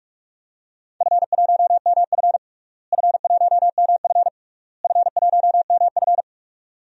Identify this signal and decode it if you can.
Morse code, heard as a keyed tone, about 46 wpm, 700 Hz. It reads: F1MF F1MF F1MF
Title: DXCC-100 Call Signs - 4 Characters - Repeated 3x - 45wpm